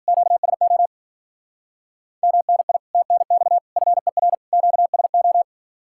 Morse code, heard as a keyed tone, about 48 wpm, 700 Hz. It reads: QSO MNI TNX FER QSO
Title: Speed-Racing Common QSO Phrases - 40wpm